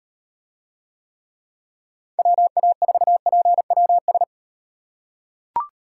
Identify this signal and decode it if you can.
Morse code, heard as a keyed tone, about 38 wpm, 700 Hz. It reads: WA4PWS